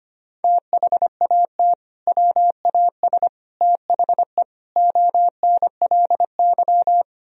Morse code, heard as a keyed tone, about 25 wpm, 700 Hz. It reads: THAT WAS THE ONLY
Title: Sentences from Top 1000 Words - Review Part 1 - 25wpm